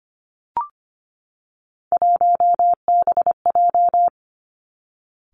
Morse code, heard as a keyed tone, about 25 wpm, 700 Hz. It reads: 1BJ
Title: Three Challenging Letter-Number Combinations 25wpm